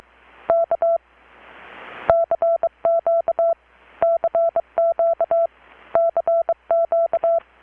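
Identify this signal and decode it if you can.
Morse code, heard as a keyed tone, about 23 wpm, 670 Hz. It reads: K CQ CQ CQ
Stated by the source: Morse from KPH's Night of Nights XX maritime radiotelegraphy event, received in Chiapas, Mexico about 3,800 km away on a Kenwood R-600 with a grounded 10 m long-wire antenna.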